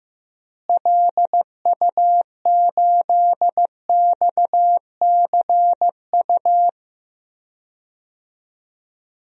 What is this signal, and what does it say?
Morse code, heard as a keyed tone, about 15 wpm, 700 Hz. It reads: LU8XCU
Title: DXCC-100 Call Signs - 6 Characters - Repeated 2x - 15wpm